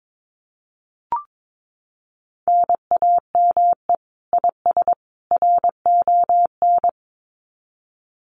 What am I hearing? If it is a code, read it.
Morse code, heard as a keyed tone, about 22 wpm, 700 Hz. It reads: NAME IS RON
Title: Common QSO Phrases 22wpm